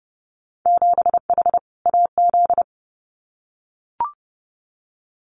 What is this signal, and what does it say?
Morse code, heard as a keyed tone, about 30 wpm, 700 Hz. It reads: 7H AZ
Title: Winter Field Day Exchanges Encore 30wpm